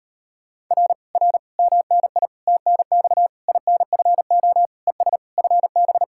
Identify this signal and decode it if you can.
Morse code, heard as a keyed tone, about 38 wpm, 700 Hz. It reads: R R MNI TNX INFO ES FB